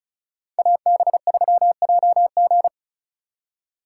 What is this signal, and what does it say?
Morse code, heard as a keyed tone, about 43 wpm, 700 Hz. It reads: AB3JG
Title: Speed Racing - USA 2x2 Call Signs 35wpm